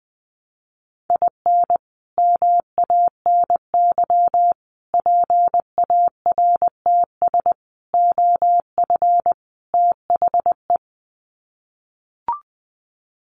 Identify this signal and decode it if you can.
Morse code, heard as a keyed tone, about 20 wpm, 700 Hz. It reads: IN MANY PARTS OF THE
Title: Sets of 5 Words Encore 20wpm